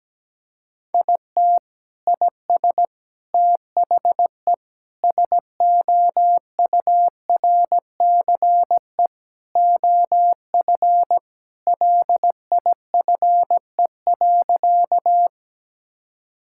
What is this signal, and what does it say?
Morse code, heard as a keyed tone, about 17 wpm, 700 Hz. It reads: IT IS THE SOURCE OF LIFE.
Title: Sentences from Top 1000 Words - Review Part 5 - 17wpm